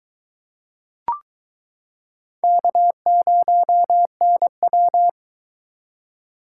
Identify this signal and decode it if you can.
Morse code, heard as a keyed tone, about 22 wpm, 700 Hz. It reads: K0NW